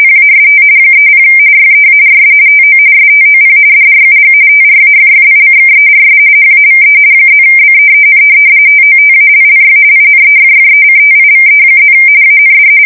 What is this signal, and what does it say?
RTTY, about 45.45 baud. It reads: PAST 30 YEARS TO STOP DRUG TRAFFICKERS HAVE +BEEN TOTALLY INEFFECTIVE+ AND
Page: 170 Shift 2125/2295